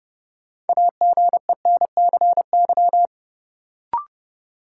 Morse code, heard as a keyed tone, about 32 wpm, 700 Hz. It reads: AGENCY
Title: Sentences from Top 1300 Words - New Word Drill - 30wpm